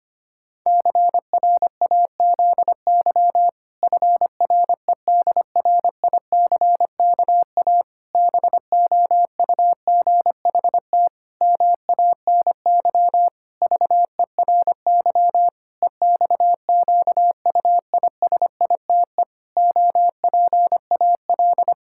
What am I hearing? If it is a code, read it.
Morse code, heard as a keyed tone, about 25 wpm, 700 Hz. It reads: CRAZY FREDRICKA BOUGHT MANY VERY EXQUISITE OPAL
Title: Pangram Challenge v2 - 25wpm